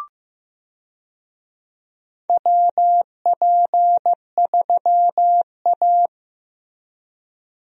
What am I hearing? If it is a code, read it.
Morse code, heard as a keyed tone, about 15 wpm, 700 Hz. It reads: WP3A
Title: USA Call Signs - 4 Characters - Repeated 2x - 15wpm